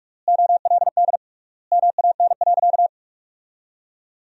Morse code, heard as a keyed tone, about 46 wpm, 700 Hz. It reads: OLD MAN.
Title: 3x Word-Spacing - Sentences from Top 500 Words - 45wpm